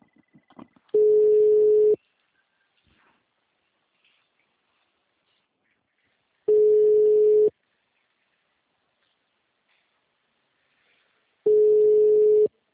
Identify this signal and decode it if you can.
Ringback tone (Europe, 425 Hz).